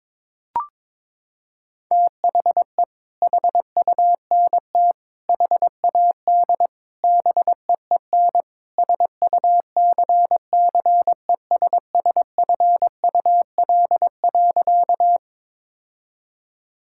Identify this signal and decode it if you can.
Morse code, heard as a keyed tone, about 22 wpm, 700 Hz. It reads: THE HUNT HAD BEEN SUCCESSFUL.
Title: Sentences from Top 1600 Words 22wpm